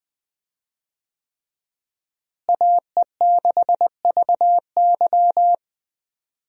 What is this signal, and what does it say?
Morse code, heard as a keyed tone, about 20 wpm, 700 Hz. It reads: AE6VY
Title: USA Call Signs - 5 Characters - Repeated 3x - 20wpm